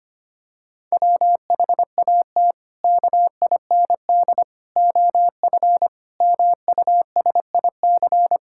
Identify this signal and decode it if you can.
Morse code, heard as a keyed tone, about 25 wpm, 700 Hz. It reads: WHAT KIND OF MUSIC
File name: practice / 25wpm / Sentences with 3-Word Phrases - 25wpm